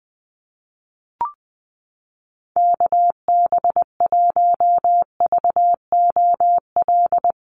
Morse code, heard as a keyed tone, about 20 wpm, 700 Hz. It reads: KB1VOL
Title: USA Call Signs Repeated 3x 20wpm